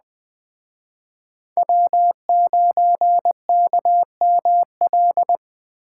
Morse code, heard as a keyed tone, about 20 wpm, 700 Hz. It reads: W9KML